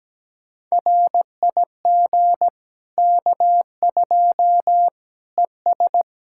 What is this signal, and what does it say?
Morse code, heard as a keyed tone, about 17 wpm, 700 Hz. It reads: RIG K2 ES
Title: Practice QSO - Exchange 2 - Info - 17wpm